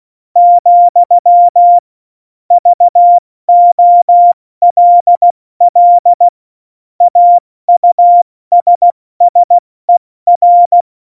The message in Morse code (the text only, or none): , VOLL AUSSER